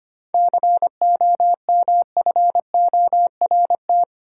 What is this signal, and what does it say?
Morse code, heard as a keyed tone, about 25 wpm, 700 Hz. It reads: COMFORT
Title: Seven Letter Words - 25wpm